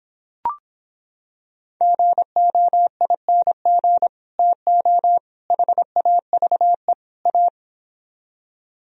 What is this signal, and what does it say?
Morse code, heard as a keyed tone, about 26 wpm, 700 Hz. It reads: GOING TO HAVE A